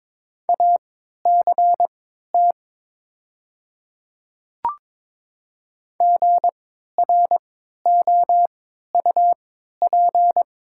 Morse code, heard as a keyed tone, about 22 wpm, spaced as for 12 wpm, 700 Hz. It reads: ACT GROUP
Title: 3x Character Spacing - Top 300 Words - 22wpm